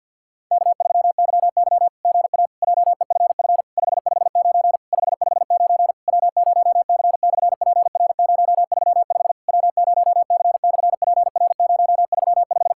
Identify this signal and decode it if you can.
Morse code, heard as a keyed tone, about 50 wpm, 700 Hz. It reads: K3YY GA JEFF 559 559 W0C/PR035 W0C/PR035
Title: SOTA QSO Practice 50wpm